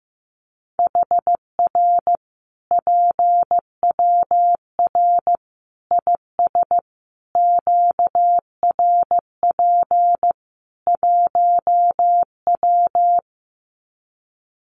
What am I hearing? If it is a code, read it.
Morse code, heard as a keyed tone, about 15 wpm, 700 Hz. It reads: HR PWR IS QRP 1W
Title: Common QSO Phrases 15wpm